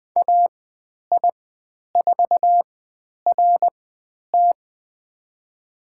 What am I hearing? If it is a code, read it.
Morse code, heard as a keyed tone, about 20 wpm, spaced as for 10 wpm, 700 Hz. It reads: AI4RT
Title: 095 Lesson 16g - Letter c - Call Signs 20wpm